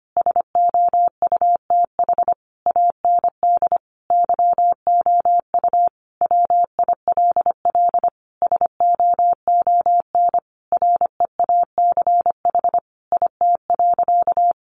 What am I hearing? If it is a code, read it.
Morse code, heard as a keyed tone, about 25 wpm, 700 Hz. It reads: SOUTH AND YOU WILL SOON REACH IT.
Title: Sentences from Top 400 Words 25wpm